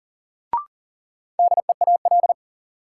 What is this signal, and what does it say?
Morse code, heard as a keyed tone, about 40 wpm, 700 Hz. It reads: DEAL
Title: Sets of 3 Words - New Word Drill - 40wpm